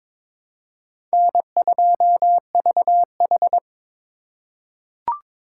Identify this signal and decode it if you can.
Morse code, heard as a keyed tone, about 22 wpm, 700 Hz. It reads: N2VH
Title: USA Call Signs - 4 Characters - 2x Character Spacing - 22wpm